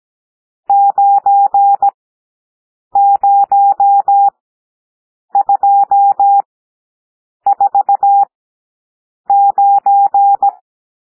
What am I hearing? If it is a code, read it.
Morse code, heard as a keyed tone, about 17 wpm, spaced as for 7 wpm, 800 Hz. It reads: 90249
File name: SOUND NUMBERS 17 wpm